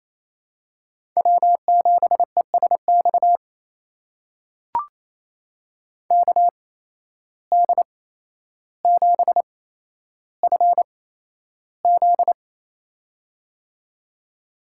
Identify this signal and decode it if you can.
Morse code, heard as a keyed tone, about 28 wpm, spaced as for 12 wpm, 700 Hz. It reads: W7ESX KD7FZ
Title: USA Call Signs - 5 Characters - 8x Character Spacing - 28wpm